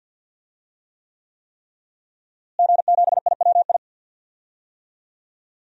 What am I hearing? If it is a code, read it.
Morse code, heard as a keyed tone, about 51 wpm, 700 Hz. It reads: G7IWI
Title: DXCC-100 Call Signs - 5 Characters - Repeated 3x - 50wpm